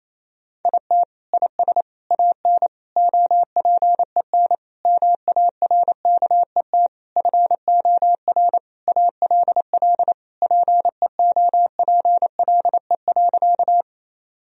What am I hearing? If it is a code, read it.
Morse code, heard as a keyed tone, about 28 wpm, 700 Hz. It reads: IT IS AN OPEN MARKET FOR ALL PEOPLE.
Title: Sentences from Top 1000 Words - Review Part 3 - 28wpm